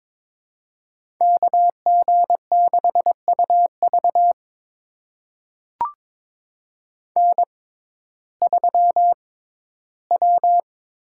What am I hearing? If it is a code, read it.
Morse code, heard as a keyed tone, about 22 wpm, spaced as for 14 wpm, 700 Hz. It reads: KG6UV N3W
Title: USA Call Signs - 5 Characters - 6x Character Spacing - 22wpm